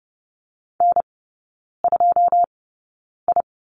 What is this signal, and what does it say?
Morse code, heard as a keyed tone, about 30 wpm, spaced as for 9 wpm, 700 Hz. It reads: N2I
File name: course / 30:10wpm / 087 Lesson 15e - Number 5 - Three Characters 30wpm